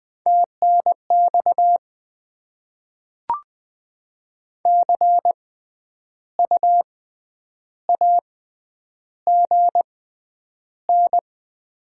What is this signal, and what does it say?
Morse code, heard as a keyed tone, about 20 wpm, 700 Hz. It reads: T N X C U A G N
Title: Top 20 QSO Elements - 6x Character Spacing - 20wpm